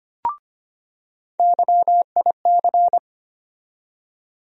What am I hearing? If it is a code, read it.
Morse code, heard as a keyed tone, about 25 wpm, 700 Hz. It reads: YIC
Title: USA Call Sign Postfixes - 3 Letters 25wpm